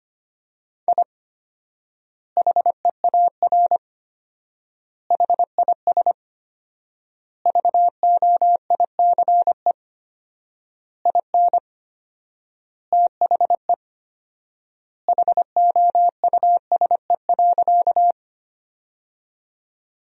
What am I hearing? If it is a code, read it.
Morse code, heard as a keyed tone, about 25 wpm, 700 Hz. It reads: I HEAR HIS VOICE IN THE HOUSE.
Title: 4x Word-Spacing - Sentences from Top 500 Words - 25wpm